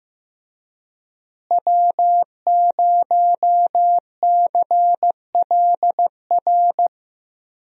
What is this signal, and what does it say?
Morse code, heard as a keyed tone, about 15 wpm, 700 Hz. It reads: W0CLR
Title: USA Call Signs - 5 Characters - Repeated 3x - 15wpm